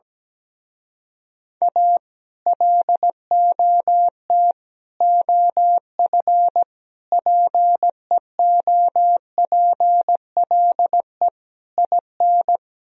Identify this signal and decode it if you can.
Morse code, heard as a keyed tone, about 17 wpm, 700 Hz. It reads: A LOT OF PEOPLE IN
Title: Speed-Racing - Sets of 5 Words 17wpm